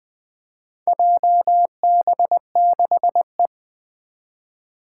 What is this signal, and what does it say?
Morse code, heard as a keyed tone, about 20 wpm, 700 Hz. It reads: JB6E
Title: Four Challenging Letter-Number Combinations 20wpm